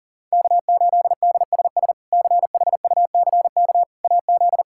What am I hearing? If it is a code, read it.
Morse code, heard as a keyed tone, about 37 wpm, 700 Hz. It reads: K8DSS CHUCK AZ
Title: K1USN SST QSO Practice - 40wpm